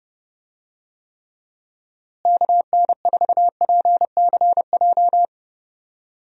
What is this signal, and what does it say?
Morse code, heard as a keyed tone, about 30 wpm, 700 Hz. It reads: KN4PCJ
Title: USA Call Signs - 6 Characters - Repeated 3x - 30wpm